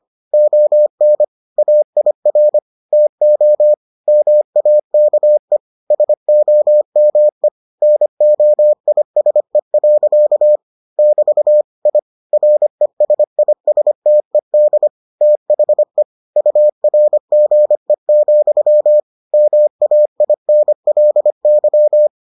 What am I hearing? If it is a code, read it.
Morse code, heard as a keyed tone, about 25 wpm, 600 Hz. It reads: ON AIR TO MAKE SOME NOISE. <BT> I RESISTED THE URGE, MAINLY